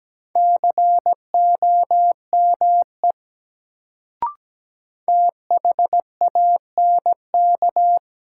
Morse code, heard as a keyed tone, about 17 wpm, 700 Hz. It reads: COME THANK
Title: Sets of 2 Words - New Word Drill - 17wpm